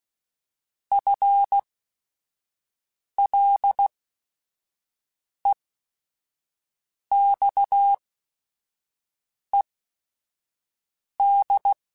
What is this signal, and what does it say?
Morse code, heard as a keyed tone, about 16 wpm, spaced as for 5 wpm, 800 Hz. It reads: FLEXED
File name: FL / FL mp3s / Fast FL @5